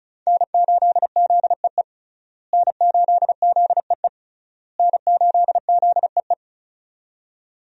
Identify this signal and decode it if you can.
Morse code, heard as a keyed tone, about 34 wpm, 700 Hz. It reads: N8ZEE N8ZEE N8ZEE
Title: USA Call Signs - 5 Characters - Repeated 3x - 35wpm